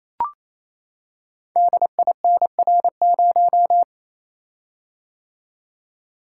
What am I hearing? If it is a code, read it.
Morse code, heard as a keyed tone, about 29 wpm, 700 Hz. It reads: DINR0
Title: Code Groups - 5 Letter:Numbers - 28wpm